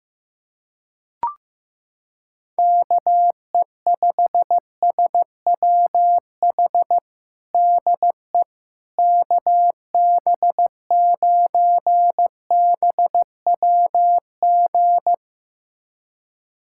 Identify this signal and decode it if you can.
Morse code, heard as a keyed tone, about 16 wpm, 700 Hz. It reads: KE5SWH DE KB9BWG
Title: USA Callsign DE Callsign - 6 Characters - 15wpm